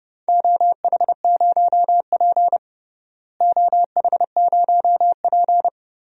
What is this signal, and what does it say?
Morse code, heard as a keyed tone, about 32 wpm, 700 Hz. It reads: OH0P OH0P
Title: DXCC-100 Call Signs - 4 Characters - Repeated 2x - 30wpm